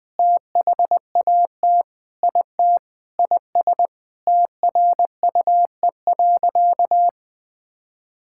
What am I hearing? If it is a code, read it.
Morse code, heard as a keyed tone, about 20 wpm, 700 Hz. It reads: THAT IT IS TRUE.
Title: Sentences with 6-Word Phrases - 20wpm